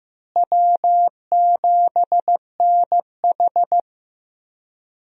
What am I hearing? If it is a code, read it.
Morse code, heard as a keyed tone, about 15 wpm, 700 Hz. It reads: W7NH
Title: USA Call Signs - 4 Characters - Repeated 3x - 15wpm